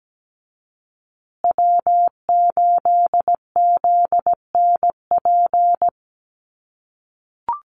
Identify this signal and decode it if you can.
Morse code, heard as a keyed tone, about 17 wpm, 700 Hz. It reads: W8ZNP
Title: USA Call Signs Repeated 3x 17wpm